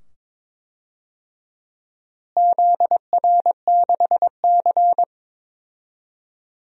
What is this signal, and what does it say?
Morse code, heard as a keyed tone, about 22 wpm, 700 Hz. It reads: ZR6C
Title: DXCC-100 Call Signs - 4 Characters - Repeated 2x - 22wpm